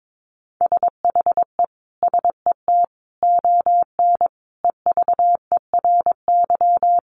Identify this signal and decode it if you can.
Morse code, heard as a keyed tone, about 22 wpm, 700 Hz. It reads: SHE SET ON EVERY
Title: Sentences from Top 1300 Words 22wpm